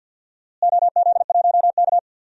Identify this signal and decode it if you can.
Morse code, heard as a keyed tone, about 51 wpm, 700 Hz. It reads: OZ1K
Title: DXCC-100 Call Signs - 4 Characters - Repeated 3x - 50wpm